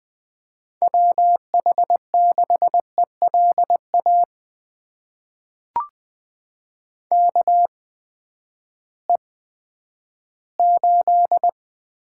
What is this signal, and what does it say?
Morse code, heard as a keyed tone, about 20 wpm, spaced as for 11 wpm, 700 Hz. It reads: WH6ELA KE8
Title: USA Call Signs - 6 Characters - 8x Character Spacing - 20wpm